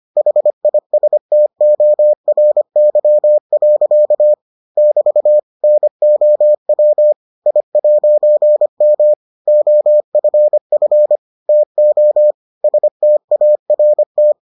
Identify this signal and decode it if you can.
Morse code, heard as a keyed tone, about 25 wpm, 600 Hz. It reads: HISTORY. <BT> NOW I'M OFF TO START